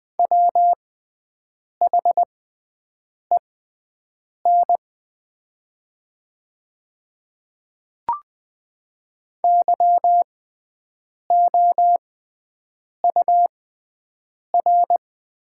Morse code, heard as a keyed tone, about 20 wpm, spaced as for 7 wpm, 690 Hz. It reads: WHEN YOUR
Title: 6x Character Spacing - Top 100 Words - 20wpm